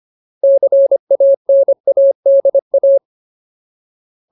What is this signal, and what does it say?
Morse code, heard as a keyed tone, about 25 wpm, 550 Hz. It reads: CANADA
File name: Ultimate Word List / 25WPM Ultimate